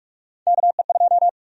Morse code, heard as a keyed tone, about 46 wpm, 700 Hz. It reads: KE2
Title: USA Call Sign Prefixes - 45wpm